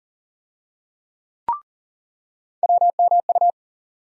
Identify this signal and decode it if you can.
Morse code, heard as a keyed tone, about 40 wpm, 700 Hz. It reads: WMU
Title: USA Call Sign Postfixes - 3 Letters 40wpm